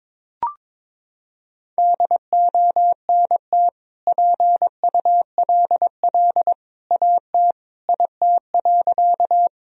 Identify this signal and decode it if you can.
Morse code, heard as a keyed tone, about 21 wpm, 700 Hz. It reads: DONT PULL AT IT.
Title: Sentences from Top 1000 Words - Review Part 2 - 22wpm